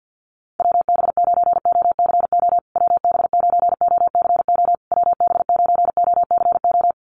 Morse code, heard as a keyed tone, about 50 wpm, 700 Hz. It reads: WB9OCO WB9OCO WB9OCO
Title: USA Call Signs Repeated 3x 50wpm